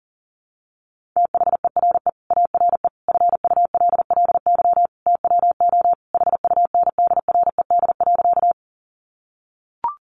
Morse code, heard as a keyed tone, about 37 wpm, 700 Hz. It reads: THERE ARE FULLY TWO HUNDRED.
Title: Sentences from Top 1300 Words 40wpm